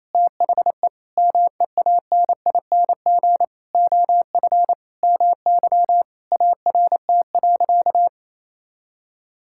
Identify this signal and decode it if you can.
Morse code, heard as a keyed tone, about 28 wpm, 700 Hz. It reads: THE MEANING OF MY ART.
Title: Sentences from Top 1000 Words - Review Part 6 - 28wpm